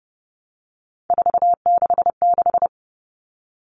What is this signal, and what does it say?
Morse code, heard as a keyed tone, about 30 wpm, 700 Hz. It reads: 466
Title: All Cut Numbers - 3-Digit 30wpm